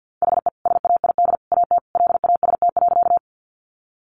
Morse code, heard as a keyed tone, about 50 wpm, 700 Hz. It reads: HE SAID AT LAST.
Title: Sentences from Top 300 Words 50wpm